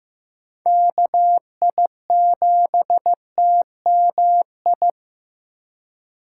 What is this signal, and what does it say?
Morse code, heard as a keyed tone, about 15 wpm, 700 Hz. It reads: KI7TMI
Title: USA Call Signs - 2x Character Spacing - 15wpm